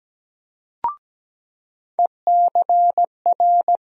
Morse code, heard as a keyed tone, about 17 wpm, 700 Hz. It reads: ECR
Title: USA Call Sign Postfixes - 3 Letters 17wpm